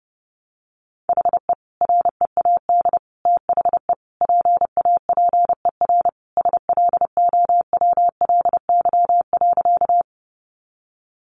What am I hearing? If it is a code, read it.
Morse code, heard as a keyed tone, about 30 wpm, 700 Hz. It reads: HE READ THE PAPER SLOWLY.
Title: Sentences from Top 1100 Words 30wpm